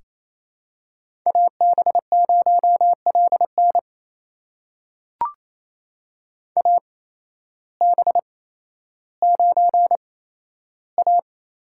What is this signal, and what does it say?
Morse code, heard as a keyed tone, about 28 wpm, spaced as for 13 wpm, 700 Hz. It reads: AB0LN AB9A
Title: USA Call Signs - 5 Characters - 8x Character Spacing - 28wpm